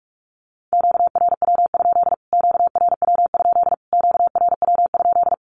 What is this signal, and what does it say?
Morse code, heard as a keyed tone, about 44 wpm, 700 Hz. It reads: QRW? QRW? QRW?
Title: Q-Codes Repeated 3x 45wpm